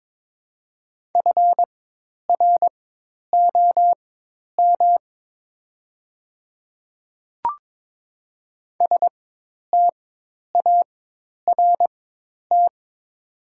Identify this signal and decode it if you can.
Morse code, heard as a keyed tone, about 22 wpm, spaced as for 10 wpm, 690 Hz. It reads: FROM START
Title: 4x Character Spacing - Top 500 Words - 22wpm